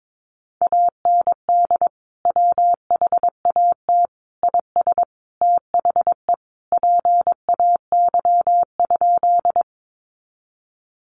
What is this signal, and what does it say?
Morse code, heard as a keyed tone, about 22 wpm, 700 Hz. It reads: AND WHAT IS THE PAY?
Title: Sentences from Top 700 Words 22wpm